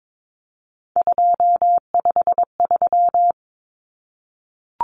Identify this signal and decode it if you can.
Morse code, heard as a keyed tone, about 22 wpm, 700 Hz. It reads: 253
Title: Numbers - 3-Digit 22wpm